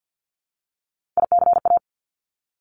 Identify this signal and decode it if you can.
Morse code, heard as a keyed tone, about 50 wpm, 700 Hz. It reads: ICA
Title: Top Three Letter Combinations in Words 50wpm